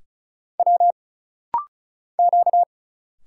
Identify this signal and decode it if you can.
Morse code, heard as a keyed tone, about 34 wpm, 700 Hz. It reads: W Q
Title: Single Letters - Rapid-Fire 35wpm